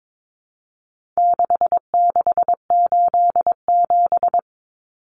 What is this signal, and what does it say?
Morse code, heard as a keyed tone, about 22 wpm, 700 Hz. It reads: 6687
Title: Common Cut Numbers - 4-Digit 22wpm